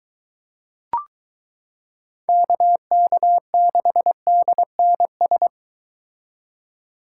Morse code, heard as a keyed tone, about 23 wpm, 700 Hz. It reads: KK6DNS